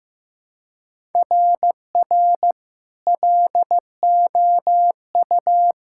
Morse code, heard as a keyed tone, about 15 wpm, 700 Hz. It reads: RR LOU
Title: SOTA QSO Practice 15wpm